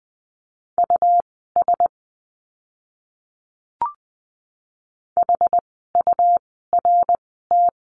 Morse code, heard as a keyed tone, about 20 wpm, spaced as for 15 wpm, 700 Hz. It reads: US HURT
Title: 101 Lesson 17f - Letter u - Words 20wpm